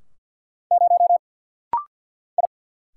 Morse code, heard as a keyed tone, about 50 wpm, 700 Hz. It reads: 0 I
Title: Single Letter-Number - Rapid-Fire 50wpm